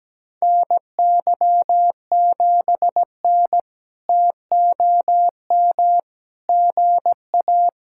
Morse code, heard as a keyed tone, about 17 wpm, 700 Hz. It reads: NY7N TOM GA
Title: K1USN SST QSO Practice - 17wpm